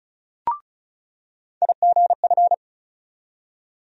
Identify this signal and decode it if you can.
Morse code, heard as a keyed tone, about 36 wpm, 700 Hz. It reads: IGF
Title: USA Call Sign Postfixes - 3 Letters 35wpm